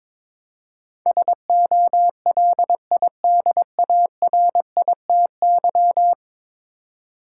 Morse code, heard as a keyed tone, about 22 wpm, 700 Hz. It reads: SOLIDARITY
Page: morse-code-ninja-practice-22wpm